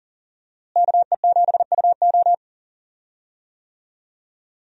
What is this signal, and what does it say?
Morse code, heard as a keyed tone, about 40 wpm, 700 Hz. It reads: KE7UO
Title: USA Call Signs - 5 Characters - Repeated 3x - 40wpm